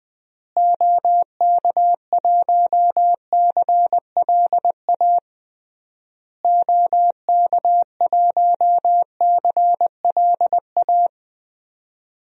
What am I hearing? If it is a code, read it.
Morse code, heard as a keyed tone, about 20 wpm, 700 Hz. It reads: OK1CLA OK1CLA
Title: DXCC-100 Call Signs - 6 Characters - Repeated 3x - 20wpm